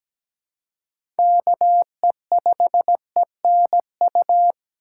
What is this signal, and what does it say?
Morse code, heard as a keyed tone, about 17 wpm, 700 Hz. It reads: KE5ENU
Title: USA Call Signs - 6 Characters - 3x Character Spacing - 17wpm